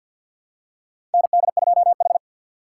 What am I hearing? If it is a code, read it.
Morse code, heard as a keyed tone, about 50 wpm, 700 Hz. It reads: ND2H
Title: USA Call Signs - 4 Characters - 3x Character Spacing - 50wpm